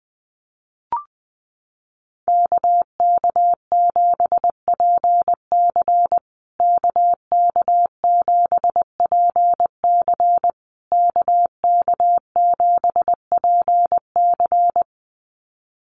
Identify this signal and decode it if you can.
Morse code, heard as a keyed tone, about 20 wpm, 700 Hz. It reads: KK7PC KK7PC KK7PC
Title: USA Call Signs Repeated 3x 20wpm